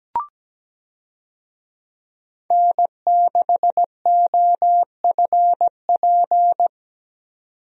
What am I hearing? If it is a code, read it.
Morse code, heard as a keyed tone, about 17 wpm, 700 Hz. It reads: N6OFP
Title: USA Call Signs - 5 Characters - 17wpm